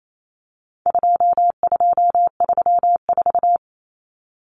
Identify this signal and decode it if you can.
Morse code, heard as a keyed tone, about 28 wpm, 700 Hz. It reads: 2234
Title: Common Cut Numbers - 4-Digit 28wpm